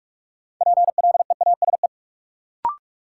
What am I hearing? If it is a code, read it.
Morse code, heard as a keyed tone, about 41 wpm, 700 Hz. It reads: PLEASE
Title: Sentences from Top 1500 Words - New Word Drill - 45wpm